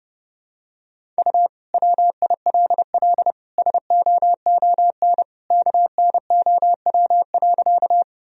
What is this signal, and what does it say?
Morse code, heard as a keyed tone, about 32 wpm, 700 Hz. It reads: U WILL SOON KNOW.
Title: Sentences from Top 400 Words and Abbreviations 30wpm